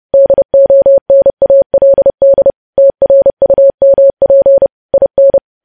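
Morse code, heard as a keyed tone, about 30 wpm, 550 Hz. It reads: DONALD TRUMP IN
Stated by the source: Morse code news headlines sent at 30 words per minute.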